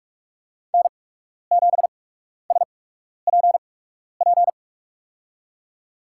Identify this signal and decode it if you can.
Morse code, heard as a keyed tone, about 46 wpm, spaced as for 12 wpm, 690 Hz. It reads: N7SPP
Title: USA Call Signs - 8x Character Spacing - 45wpm